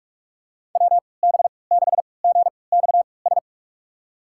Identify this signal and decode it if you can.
Morse code, heard as a keyed tone, about 47 wpm, spaced as for 26 wpm, 700 Hz. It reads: WB6GXS
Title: USA Call Signs - 6 Characters - 3x Character Spacing - 45wpm